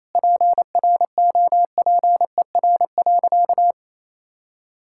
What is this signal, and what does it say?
Morse code, heard as a keyed tone, about 27 wpm, 700 Hz. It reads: PROPER.
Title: Sentences from Top 1000 Words - Review Part 5 - 28wpm